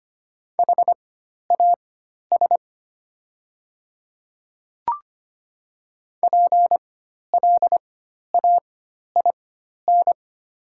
Morse code, heard as a keyed tone, about 25 wpm, spaced as for 12 wpm, 690 Hz. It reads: HAS PLAIN
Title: 4x Character Spacing - Top 500 Words - 25wpm